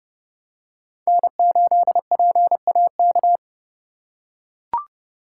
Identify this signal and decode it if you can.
Morse code, heard as a keyed tone, about 30 wpm, 700 Hz. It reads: N8PAK
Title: USA Call Signs - 5 Characters - 2x Character Spacing - 30wpm